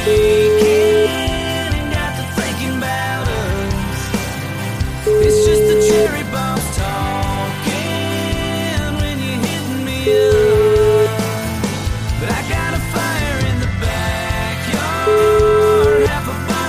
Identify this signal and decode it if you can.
Ringback tone (Europe, 425 Hz), with noise.